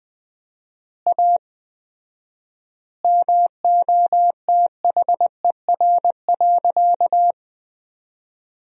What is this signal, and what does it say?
Morse code, heard as a keyed tone, about 20 wpm, 700 Hz. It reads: A MOTHER.
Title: 4x Word-Spacing - Sentences from Top 200 Words - 20wpm